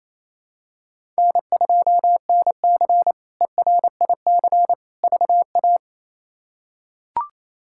Morse code, heard as a keyed tone, about 28 wpm, 700 Hz. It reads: N2NC ERIC VA
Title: K1USN SST QSO Practice - 28wpm